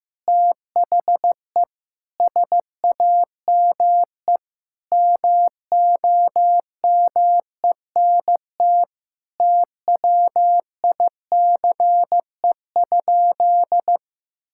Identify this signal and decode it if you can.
Morse code, heard as a keyed tone, about 15 wpm, 700 Hz. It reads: THE SAME MOMENT TWICE?
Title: Sentences from Top 1600 Words 15wpm